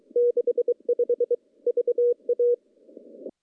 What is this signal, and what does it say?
Morse code, heard as a keyed tone, about 23 wpm, 480 Hz. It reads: 65 VA
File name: contests / CW-SS-k5tr- -0200